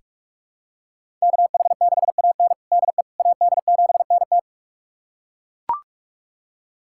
Morse code, heard as a keyed tone, about 42 wpm, 700 Hz. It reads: KH6AN DE AD7NT
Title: USA Callsign DE Callsign 45wpm